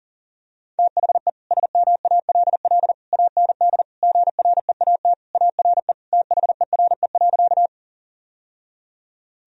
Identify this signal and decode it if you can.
Morse code, heard as a keyed tone, about 40 wpm, 700 Hz. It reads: THE SMALL AND GREAT ARE THERE.
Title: Sentences from Top 1000 Words - Review Part 1 - 40wpm